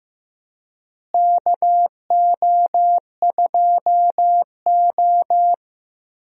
Morse code, heard as a keyed tone, about 15 wpm, 700 Hz. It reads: KO2O
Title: USA Call Signs - 4 Characters - Repeated 3x - 15wpm